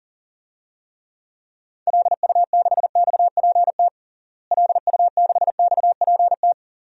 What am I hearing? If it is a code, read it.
Morse code, heard as a keyed tone, about 40 wpm, 700 Hz. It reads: LU6XPT LU6XPT
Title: DXCC-100 Call Signs - 6 Characters - Repeated 2x - 40wpm